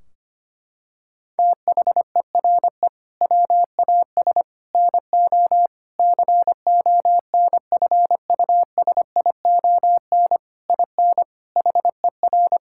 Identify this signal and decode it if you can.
Morse code, heard as a keyed tone, about 25 wpm, 700 Hz. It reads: THERE WAS NO CONFUSION IN HER
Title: Sentences from Top 1600 Words 25wpm